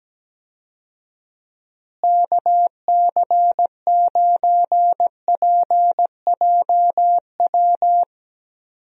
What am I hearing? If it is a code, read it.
Morse code, heard as a keyed tone, about 17 wpm, 700 Hz. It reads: KC9PJW
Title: USA Call Signs - 6 Characters - Repeated 3x - 17wpm